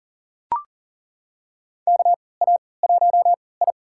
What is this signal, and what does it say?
Morse code, heard as a keyed tone, about 40 wpm, spaced as for 23 wpm, 700 Hz. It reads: KA1I
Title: USA Call Signs - 3x Character Spacing - 40wpm